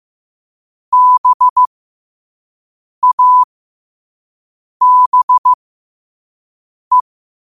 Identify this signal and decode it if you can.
Morse code, heard as a keyed tone, about 15 wpm, spaced as for 5 wpm, 1000 Hz. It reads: BABE
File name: qotdfiles / 5wpm